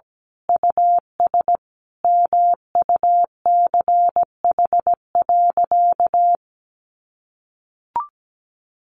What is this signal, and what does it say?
Morse code, heard as a keyed tone, about 17 wpm, 700 Hz. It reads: US MUCH.
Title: Sentences from Top 1500 Words 17wpm